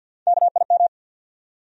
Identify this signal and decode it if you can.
Morse code, heard as a keyed tone, about 51 wpm, 700 Hz. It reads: KIM
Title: USA Common Names - Max 4 Letters - 50wpm